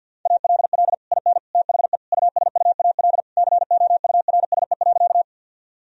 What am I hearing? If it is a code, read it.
Morse code, heard as a keyed tone, about 51 wpm, 700 Hz. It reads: ALL IN THE USUAL COURSE.
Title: Sentences from Top 1000 Words - Review Part 2 - 50wpm